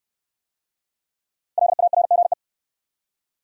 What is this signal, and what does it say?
Morse code, heard as a keyed tone, about 68 wpm, 700 Hz. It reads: LARGE